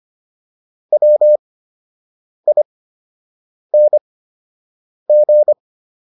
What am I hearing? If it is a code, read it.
Morse code, heard as a keyed tone, about 25 wpm, spaced as for 7 wpm, 600 Hz. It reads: WING